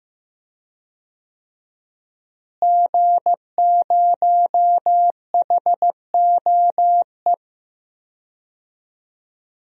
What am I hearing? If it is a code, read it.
Morse code, heard as a keyed tone, about 15 wpm, 700 Hz. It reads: G0HOE
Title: DXCC-100 Call Signs - 5 Characters - Repeated 3x - 15wpm